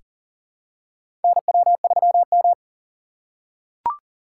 Morse code, heard as a keyed tone, about 40 wpm, 700 Hz. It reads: NW3M
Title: USA Call Signs - 4 Characters - 3x Character Spacing - 40wpm